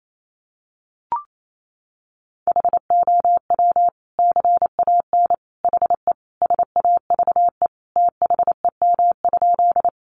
Morse code, heard as a keyed tone, about 28 wpm, 700 Hz. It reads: HOW CAN HE SAVE THEM?
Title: Sentences from Top 1100 Words 28wpm